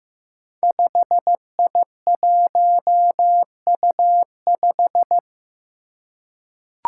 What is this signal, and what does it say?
Morse code, heard as a keyed tone, about 15 wpm, 700 Hz. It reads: 5I1U5
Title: Five Challenging Letter-Number Combinations 15wpm